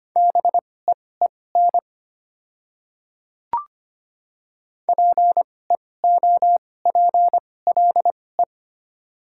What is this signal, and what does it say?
Morse code, heard as a keyed tone, about 25 wpm, spaced as for 18 wpm, 700 Hz. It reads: BEEN PEOPLE
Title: 2x Character Spacing - Top 100 Words - 25wpm